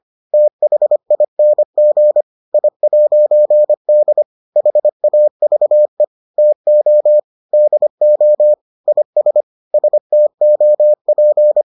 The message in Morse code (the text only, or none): THING I'D HAVE TO DO IS STOP